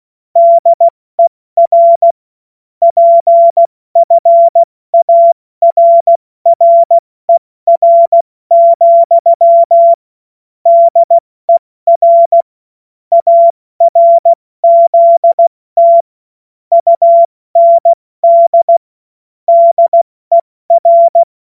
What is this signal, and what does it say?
Morse code, heard as a keyed tone, about 16 wpm, 680 Hz. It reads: DER PFARRER, DER ARZT UND DER
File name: Morsekurs / Witze / Witz 06. September 2022 70 F80